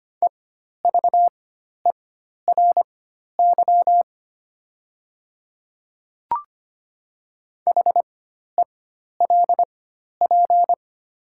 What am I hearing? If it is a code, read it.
Morse code, heard as a keyed tone, about 25 wpm, spaced as for 12 wpm, 700 Hz. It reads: EVERY HELP
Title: 4x Character Spacing - Top 200 Words - 25wpm